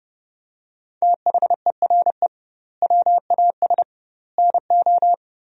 Morse code, heard as a keyed tone, about 30 wpm, 700 Hz. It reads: THERE WAS NO
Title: 2x Word-Spacing - Sentences from Top 400 Words - 30wpm